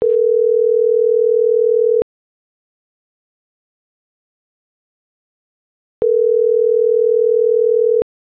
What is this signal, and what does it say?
Ringback tone (North America), 440 + 480 Hz.